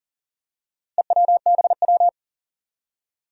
Morse code, heard as a keyed tone, about 40 wpm, 700 Hz. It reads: EWBW
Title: Four Challenging Letter-Number Combinations 40wpm